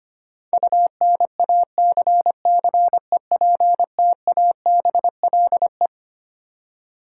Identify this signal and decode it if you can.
Morse code, heard as a keyed tone, about 25 wpm, 700 Hz. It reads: UNACCEPTABLE
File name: Twelve Letter Words - 25wpm